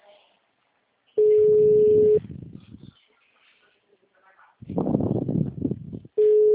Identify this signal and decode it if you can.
Ringback tone (Europe, 425 Hz), with speech.
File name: out- -2007- - - .515